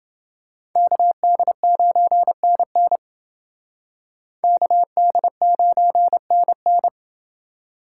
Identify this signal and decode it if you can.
Morse code, heard as a keyed tone, about 28 wpm, 700 Hz. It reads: KD9NN KD9NN